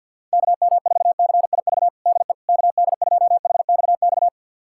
Morse code, heard as a keyed tone, about 49 wpm, 700 Hz. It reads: KM4CIV DE KD2HXX
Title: USA Callsign DE Callsign 50wpm